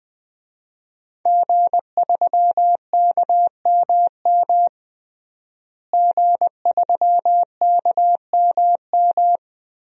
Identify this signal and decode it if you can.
Morse code, heard as a keyed tone, about 20 wpm, 700 Hz. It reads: G3KMM G3KMM
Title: DXCC-100 Call Signs - 5 Characters - Repeated 3x - 20wpm